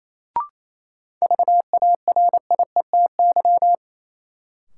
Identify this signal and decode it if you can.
Morse code, heard as a keyed tone, about 28 wpm, 700 Hz. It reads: VARIETY
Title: Sets of 3 Words - New Word Drill - 28wpm